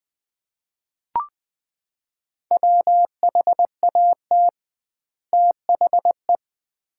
Morse code, heard as a keyed tone, about 21 wpm, 700 Hz. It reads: WHAT THE
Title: 2x Word-Spacing - Sets of 2 Words - 20wpm